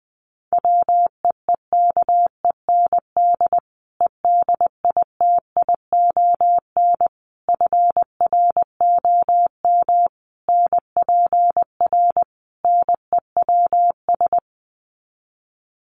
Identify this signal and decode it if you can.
Morse code, heard as a keyed tone, about 20 wpm, 700 Hz. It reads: WEEKEND EDITION FROM NPR NEWS
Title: Sets of 5 Words 20wpm